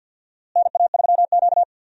Morse code, heard as a keyed tone, about 51 wpm, 700 Hz. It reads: NA3Q
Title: USA Call Signs - 4 Characters - Repeated 3x - 50wpm